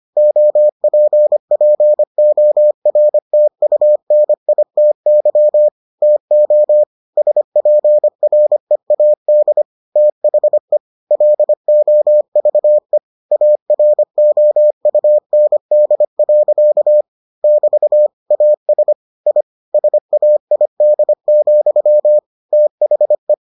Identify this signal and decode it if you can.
Morse code, heard as a keyed tone, about 25 wpm, 600 Hz. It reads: OPPORTUNITY TO SPREAD THE LOVE AROUND. <BT> AS I SAID, THE